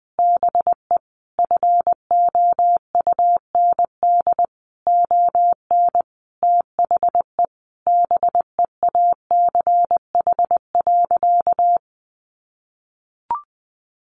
Morse code, heard as a keyed tone, about 20 wpm, 700 Hz. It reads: BE FOUND ON THE BEACH.
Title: Sentences from Top 1500 Words 20wpm